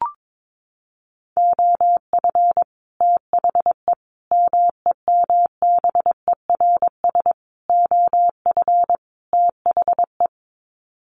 Morse code, heard as a keyed tone, about 22 wpm, 700 Hz. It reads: OF THE MEMBERS OF THE
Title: Sets of 5 Words Encore2 22wpm